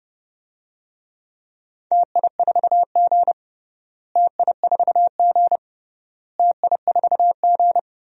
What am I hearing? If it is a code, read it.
Morse code, heard as a keyed tone, about 30 wpm, 700 Hz. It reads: TI4G TI4G TI4G
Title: DXCC-100 Call Signs - 4 Characters - Repeated 3x - 30wpm